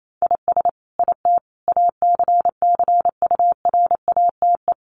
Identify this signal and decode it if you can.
Morse code, heard as a keyed tone, about 28 wpm, 700 Hz. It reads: IS IT ACCURATE
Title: Sentences from Top 1200 Words 28wpm